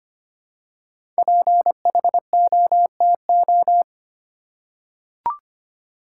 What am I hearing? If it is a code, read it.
Morse code, heard as a keyed tone, about 25 wpm, 700 Hz. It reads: PHOTO
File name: 5 Letter Words - First 3 Letters Given - 25wpm